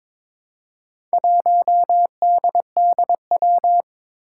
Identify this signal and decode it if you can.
Morse code, heard as a keyed tone, about 22 wpm, 700 Hz. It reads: 1DDW
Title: Four Challenging Letter-Number Combinations 22wpm